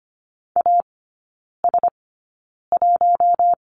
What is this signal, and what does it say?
Morse code, heard as a keyed tone, about 25 wpm, spaced as for 9 wpm, 700 Hz. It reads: AS1
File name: course / 25:10wpm / 093 Lesson 16e - Letter c - Three Characters 25wpm